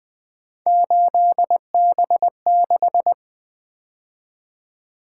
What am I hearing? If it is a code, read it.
Morse code, heard as a keyed tone, about 24 wpm, 700 Hz. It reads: 8B6
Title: Speed-Racing - Three Letter-Number Combinations - 20wpm